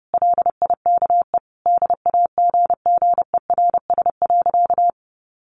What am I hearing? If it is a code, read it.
Morse code, heard as a keyed tone, about 30 wpm, 700 Hz. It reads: LIKE DAGGERS.
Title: A Sound Of Thunder 30wpm